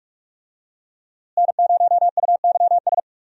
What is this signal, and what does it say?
Morse code, heard as a keyed tone, about 43 wpm, 700 Hz. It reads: N0UYS
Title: USA Call Signs - 5 Characters - Repeated 3x - 45wpm